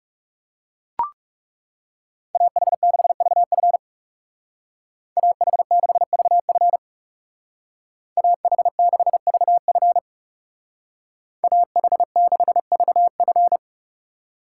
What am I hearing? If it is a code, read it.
Morse code, heard as a keyed tone, about 38 wpm, 700 Hz. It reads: AH6VF AH6VF AH6VF AH6VF